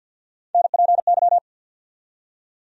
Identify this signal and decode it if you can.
Morse code, heard as a keyed tone, about 47 wpm, 700 Hz. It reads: NPY
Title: USA Call Sign Postfixes - 3 Letters 50wpm